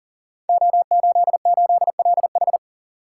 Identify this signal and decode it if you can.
Morse code, heard as a keyed tone, about 39 wpm, 690 Hz. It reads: O88LH
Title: Code Groups - 5 Letter:Numbers - 40wpm